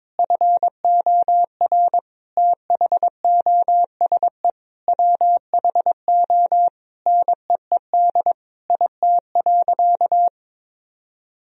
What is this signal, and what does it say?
Morse code, heard as a keyed tone, about 22 wpm, 700 Hz. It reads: FOR THOSE WHO NEED IT.
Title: Sentences from Top 1000 Words - Review Part 4 - 22wpm